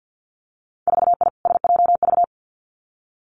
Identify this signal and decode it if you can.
Morse code, heard as a keyed tone, about 50 wpm, 700 Hz. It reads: 4I SJV
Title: Winter Field Day Exchanges Encore 50wpm